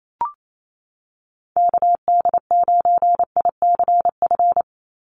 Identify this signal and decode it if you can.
Morse code, heard as a keyed tone, about 28 wpm, 700 Hz. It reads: KD9ICF
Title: USA Call Signs 28wpm